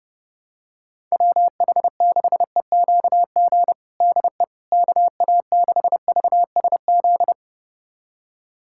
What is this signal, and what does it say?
Morse code, heard as a keyed tone, about 32 wpm, 700 Hz. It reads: WH6EQG DE KA6VSZ
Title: USA Callsign DE Callsign - 6 Characters - 30wpm